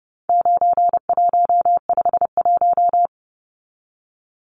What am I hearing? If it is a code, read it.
Morse code, heard as a keyed tone, about 30 wpm, 700 Hz. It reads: 9151
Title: Numbers - 4-Digit 30wpm